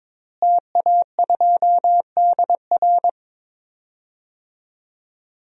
Morse code, heard as a keyed tone, about 21 wpm, 700 Hz. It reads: TA2DR
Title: DXCC-100 Call Signs - 5 Characters - Repeated 2x - 22wpm